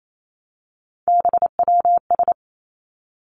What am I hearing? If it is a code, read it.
Morse code, heard as a keyed tone, about 28 wpm, 700 Hz. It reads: BWS
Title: Three Challenging Letter-Number Combinations 28wpm